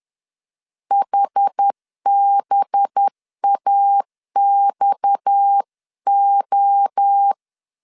Morse code, heard as a keyed tone, about 10 wpm, 800 Hz. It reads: HBAXO
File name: CW / DET / 1 6 1500C S B 10wpm 4137